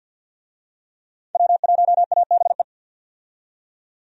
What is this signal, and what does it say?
Morse code, heard as a keyed tone, about 47 wpm, 700 Hz. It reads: W1ABE
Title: USA Call Signs - 5 Characters - 50wpm